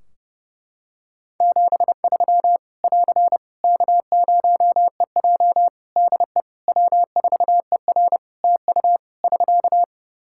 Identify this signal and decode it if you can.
Morse code, heard as a keyed tone, about 30 wpm, 700 Hz. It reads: 73 <AR> K0EJ DE W4ER TU <SK>
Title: Practice QSO - Exchange 4 - Ending - 30wpm